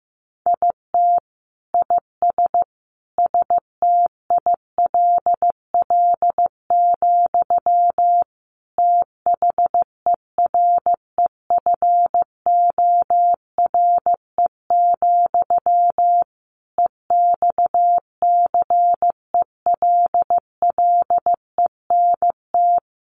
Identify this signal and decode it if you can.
Morse code, heard as a keyed tone, about 15 wpm, 700 Hz. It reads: IT IS STILL, THEREFORE, EXCELLENT
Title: Sentences from Top 1000 Words 15wpm